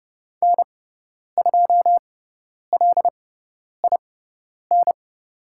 Morse code, heard as a keyed tone, about 30 wpm, spaced as for 10 wpm, 700 Hz. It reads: N2LIN
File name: Lesson 16g - Letter c - Call Signs 30wpm